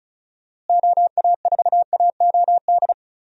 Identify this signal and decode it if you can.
Morse code, heard as a keyed tone, about 35 wpm, 700 Hz. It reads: OA4AOD
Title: USA Call Signs - 6 Characters - Repeated 2x - 35wpm